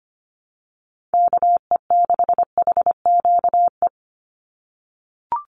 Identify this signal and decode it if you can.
Morse code, heard as a keyed tone, about 25 wpm, 700 Hz. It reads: KE6HQE
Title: USA Call Signs 25wpm